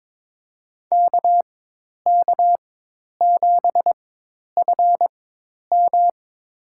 Morse code, heard as a keyed tone, about 21 wpm, spaced as for 10 wpm, 700 Hz. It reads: KK7FM
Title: USA Call Signs - 5 Characters - 4x Character Spacing - 22wpm